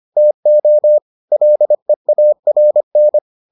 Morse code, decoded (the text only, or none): TO LEARN